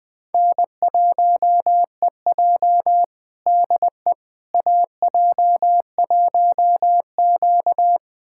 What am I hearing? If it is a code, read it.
Morse code, heard as a keyed tone, about 20 wpm, 700 Hz. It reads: N1EJ DE AJ1Q
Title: USA Callsign DE Callsign - 4 Characters - 20wpm